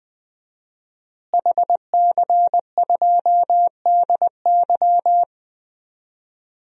Morse code, heard as a keyed tone, about 21 wpm, 700 Hz. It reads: HC2DY
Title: DXCC-100 Call Signs - 5 Characters - Repeated 3x - 20wpm